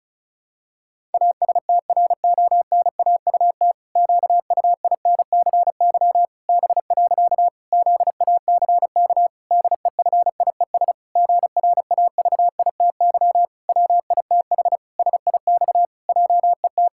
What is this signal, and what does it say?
Morse code, heard as a keyed tone, about 36 wpm, 700 Hz. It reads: ASTRONAUT QUINCY B. ZACK DEFIES GRAVITY WITH SIX JET
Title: Pangram Challenge v2 - 35wpm